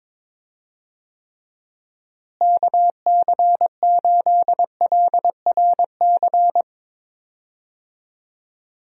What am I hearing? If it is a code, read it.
Morse code, heard as a keyed tone, about 22 wpm, 700 Hz. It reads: KC8LRC
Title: USA Call Signs - 6 Characters - Repeated 3x - 22wpm